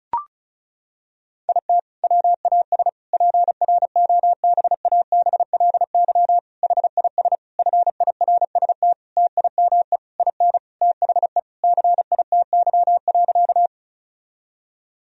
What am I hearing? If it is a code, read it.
Morse code, heard as a keyed tone, about 36 wpm, 700 Hz. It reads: IT WAS PROBABLY HIS FIRST TIME IN THE CITY.
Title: Sentences from Top 1000 Words - Review Part 4 - 35wpm